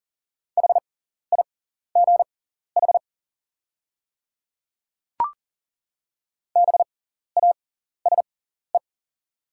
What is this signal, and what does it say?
Morse code, heard as a keyed tone, about 40 wpm, spaced as for 14 wpm, 700 Hz. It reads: HIGH BASE
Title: 6x Character Spacing - Top 300 Words - 40wpm